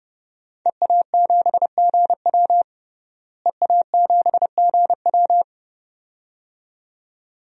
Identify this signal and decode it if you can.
Morse code, heard as a keyed tone, about 32 wpm, 700 Hz. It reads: EA7GW EA7GW
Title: DXCC-100 Call Signs - 5 Characters - Repeated 2x - 30wpm